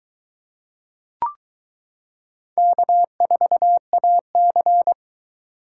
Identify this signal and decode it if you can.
Morse code, heard as a keyed tone, about 23 wpm, 700 Hz. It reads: K4AC